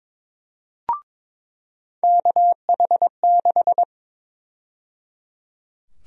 Morse code, heard as a keyed tone, about 22 wpm, 700 Hz. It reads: KH6
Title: USA Call Sign Prefixes - 22wpm